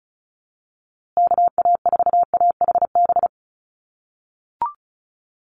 Morse code, heard as a keyed tone, about 34 wpm, 700 Hz. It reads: KA4AHB
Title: USA Call Signs Repeated 2x 35wpm